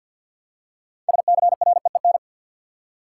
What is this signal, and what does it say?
Morse code, heard as a keyed tone, about 53 wpm, 700 Hz. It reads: SCREEN
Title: Six Letter Words - 50wpm